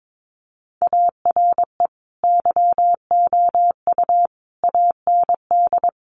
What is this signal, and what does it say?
Morse code, heard as a keyed tone, about 22 wpm, 700 Hz. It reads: ARE YOU AND
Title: Sentences from Top 1400 Words 22wpm